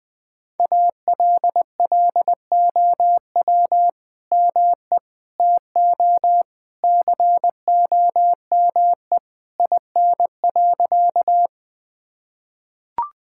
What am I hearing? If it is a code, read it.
Morse code, heard as a keyed tone, about 20 wpm, 700 Hz. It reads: ALLOW ME TO COME IN.
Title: Sentences from Top 1000 Words - Review Part 5 - 20wpm